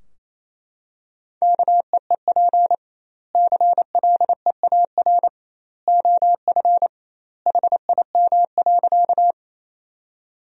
Morse code, heard as a keyed tone, about 28 wpm, 700 Hz. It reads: KEEP CLEAR OF HIM.
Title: 2x Word-Spacing - Sentences from Top 500 Words - 28wpm